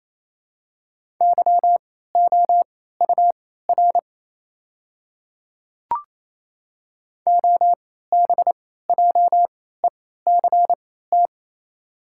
Morse code, heard as a keyed tone, about 28 wpm, spaced as for 16 wpm, 690 Hz. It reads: YOUR OBJECT
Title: 3x Character Spacing - Top 500 Words - 28wpm